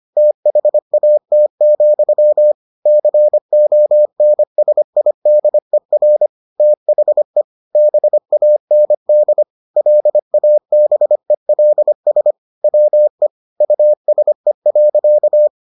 Morse code, heard as a keyed tone, about 25 wpm, 600 Hz. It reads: THAT, CONSIDER THE BAND LABELS WE USE.